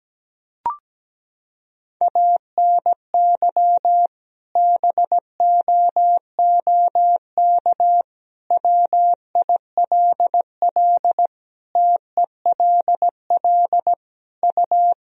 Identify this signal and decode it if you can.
Morse code, heard as a keyed tone, about 17 wpm, 700 Hz. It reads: ANY BOOK WILL TELL U
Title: Sentences from Top 300 Words and Abbreviations 17wpm